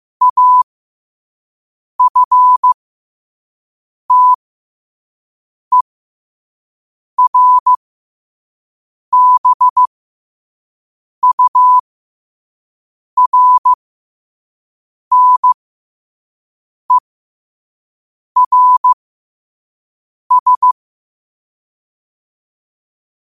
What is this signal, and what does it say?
Morse code, heard as a keyed tone, about 15 wpm, spaced as for 5 wpm, 1000 Hz. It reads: AFTERBURNERS
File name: qotdfiles / 5wpm